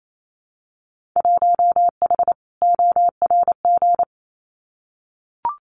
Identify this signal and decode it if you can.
Morse code, heard as a keyed tone, about 28 wpm, 700 Hz. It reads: 1H ORG
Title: Winter Field Day Exchanges 28wpm